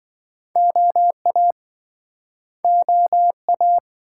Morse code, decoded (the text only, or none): OA OA